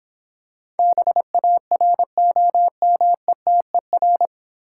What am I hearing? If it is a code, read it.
Morse code, heard as a keyed tone, about 26 wpm, 700 Hz. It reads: BAROMETER